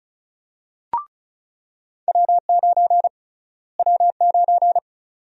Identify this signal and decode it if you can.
Morse code, heard as a keyed tone, about 34 wpm, 700 Hz. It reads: W9 W9
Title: USA Call Sign Prefixes - Repeated 2x - 35wpm